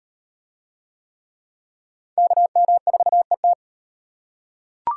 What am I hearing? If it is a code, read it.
Morse code, heard as a keyed tone, about 38 wpm, 700 Hz. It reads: KM4ET